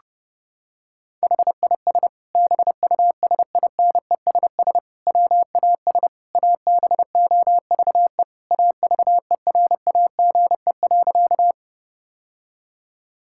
Morse code, heard as a keyed tone, about 31 wpm, 700 Hz. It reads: HIS BUSINESS WAS ABOVE AVERAGE.
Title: Sentences from Top 1000 Words - Review Part 6 - 30wpm